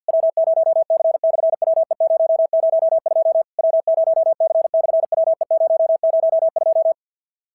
Morse code, heard as a keyed tone, about 52 wpm, 690 Hz. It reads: W0C/WE002 W0C/WE002
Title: SOTA QSO Practice 50wpm